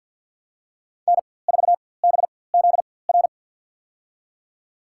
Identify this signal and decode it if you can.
Morse code, heard as a keyed tone, about 49 wpm, spaced as for 23 wpm, 700 Hz. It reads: N4BZR
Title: USA Call Signs - 5 Characters - 4x Character Spacing - 50wpm